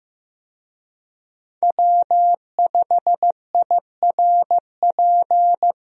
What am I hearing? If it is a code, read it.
Morse code, heard as a keyed tone, about 15 wpm, 700 Hz. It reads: W5IRP
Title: USA Call Signs - 5 Characters - Repeated 3x - 15wpm